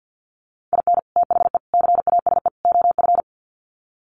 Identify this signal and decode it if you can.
Morse code, heard as a keyed tone, about 50 wpm, 700 Hz. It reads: IN THE CASE OF
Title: Sets of 4 Words 50wpm